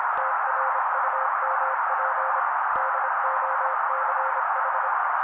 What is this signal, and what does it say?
Morse code, heard as a keyed tone, about 25 wpm, 530 Hz. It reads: TRUMP DOCH
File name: daily / 25wpm noise